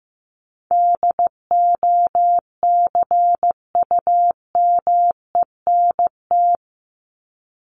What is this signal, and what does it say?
Morse code, heard as a keyed tone, about 15 wpm, 700 Hz. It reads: DOCUMENT
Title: Top 1751-2000 Words 15wpm